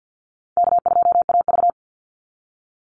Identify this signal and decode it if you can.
Morse code, heard as a keyed tone, about 50 wpm, 700 Hz. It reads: K2AV
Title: Easy Call Signs - 50wpm